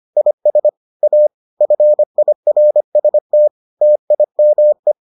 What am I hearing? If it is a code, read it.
Morse code, heard as a keyed tone, about 25 wpm, 600 Hz. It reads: IS A FIRST TIME